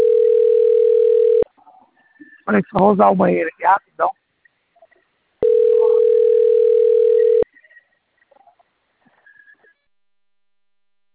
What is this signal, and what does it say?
Ringback tone (North America), 440 + 480 Hz, with speech.
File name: rg-1600-unknown- - - .77663